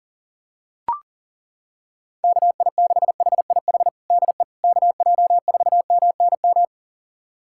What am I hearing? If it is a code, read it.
Morse code, heard as a keyed tone, about 40 wpm, 700 Hz. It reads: KI6HIH DE KJ4MNM
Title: USA Callsign DE Callsign 40wpm